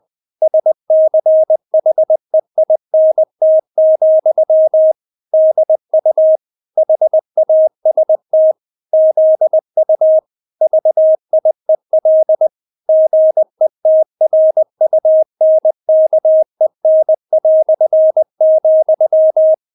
Morse code, heard as a keyed tone, about 20 wpm, 630 Hz. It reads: SCHEINT, DU HAST ZU VIEL GETRUNKEN",
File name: Witz1 20wpm